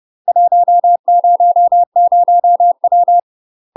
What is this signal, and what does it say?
Morse code, heard as a keyed tone, about 30 wpm, 700 Hz. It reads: 100W
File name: UK MASTER QSOs TRAINING / 30WPM / UK MASTER QSOs 0040